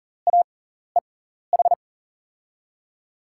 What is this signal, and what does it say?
Morse code, heard as a keyed tone, about 40 wpm, spaced as for 14 wpm, 700 Hz. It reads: AEH
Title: USA Call Signs - 5 Characters - 6x Character Spacing - 40wpm